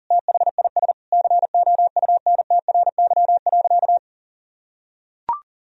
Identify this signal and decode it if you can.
Morse code, heard as a keyed tone, about 40 wpm, 700 Hz. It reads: THIS COUNTRY.
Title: Sentences from Top 1000 Words - Review Part 3 - 40wpm